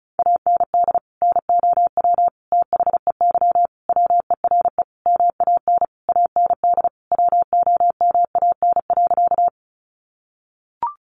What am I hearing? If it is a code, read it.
Morse code, heard as a keyed tone, about 34 wpm, 700 Hz. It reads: AND NOW THEY WERE MAN AND WOMAN.
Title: Sentences from Top 1100 Words 35wpm